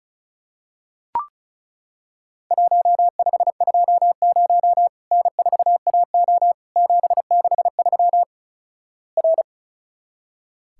Morse code, heard as a keyed tone, about 34 wpm, 700 Hz. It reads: 1520 N4AO 763 R
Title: Worked All Europe DX Contest - QTC Exchange - 35wpm